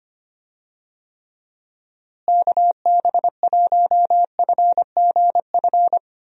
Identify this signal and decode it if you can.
Morse code, heard as a keyed tone, about 25 wpm, 700 Hz. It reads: KB1FGF
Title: USA Call Signs - 6 Characters - Repeated 2x - 25wpm